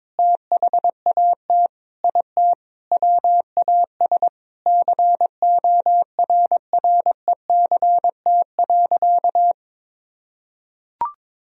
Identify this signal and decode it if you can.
Morse code, heard as a keyed tone, about 22 wpm, 700 Hz. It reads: THAT IT WAS CORRECT.
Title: Sentences from Top 1000 Words - Review Part 2 - 22wpm